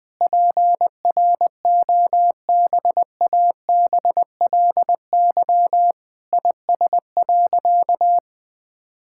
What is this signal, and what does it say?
Morse code, heard as a keyed tone, about 21 wpm, 700 Hz. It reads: PROBABLY IS.
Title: Sentences with 9-Word Phrases - 20wpm